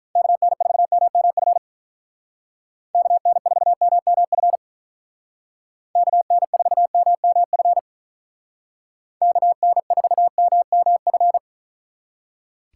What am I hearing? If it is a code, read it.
Morse code, heard as a keyed tone, about 42 wpm, 700 Hz. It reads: KN4MMF KN4MMF KN4MMF KN4MMF
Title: Speed Racing - USA 2x3 Call Signs 35wpm